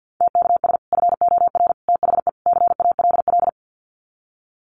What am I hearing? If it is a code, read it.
Morse code, heard as a keyed tone, about 50 wpm, 700 Hz. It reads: TKS FOR THE CALL
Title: Common QSO Phrases 50wpm